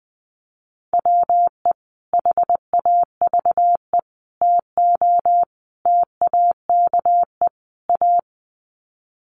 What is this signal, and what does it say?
Morse code, heard as a keyed tone, about 20 wpm, 700 Hz. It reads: WE HAVE TO TAKE A
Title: Sets of 5 Words Encore 20wpm